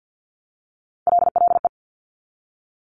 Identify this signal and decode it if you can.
Morse code, heard as a keyed tone, about 50 wpm, 700 Hz. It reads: LLE
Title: Top Three Letter Combinations in Words 50wpm